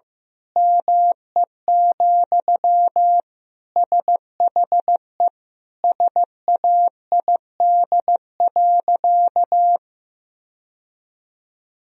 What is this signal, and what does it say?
Morse code, heard as a keyed tone, about 15 wpm, 700 Hz. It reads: ME, SHE SAID.
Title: Sentences from Top 1000 Words - Review Part 3 - 15wpm